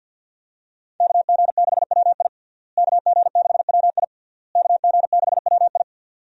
Morse code, heard as a keyed tone, about 48 wpm, 700 Hz. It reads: KG6WI KG6WI KG6WI
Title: USA Call Signs - 5 Characters - Repeated 3x - 50wpm